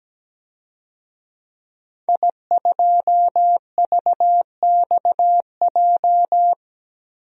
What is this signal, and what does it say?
Morse code, heard as a keyed tone, about 17 wpm, 700 Hz. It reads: I2VXJ
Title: DXCC-100 Call Signs - 5 Characters - 17wpm